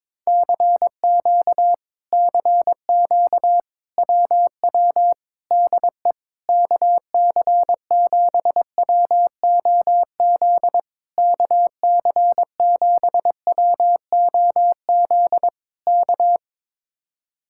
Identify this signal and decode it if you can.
Morse code, heard as a keyed tone, about 22 wpm, 700 Hz. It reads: CQ CQ WW DE KC7WOZ KC7WOZ K
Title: Speed-Racing Common QSO Phrases - 22wpm